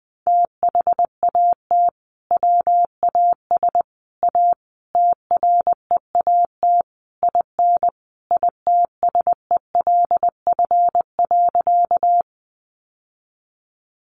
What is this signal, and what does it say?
Morse code, heard as a keyed tone, about 20 wpm, 700 Hz. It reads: THAT WAS A TREAT IN ITSELF.
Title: Sentences from Top 1300 Words 20wpm